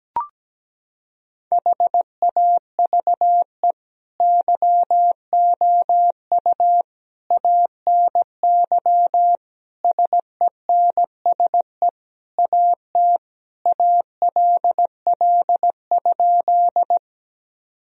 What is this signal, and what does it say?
Morse code, heard as a keyed tone, about 17 wpm, 700 Hz. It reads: HAVE YOU ANY SENSE AT ALL?
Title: Sentences from Top 1000 Words - Review Part 4 - 17wpm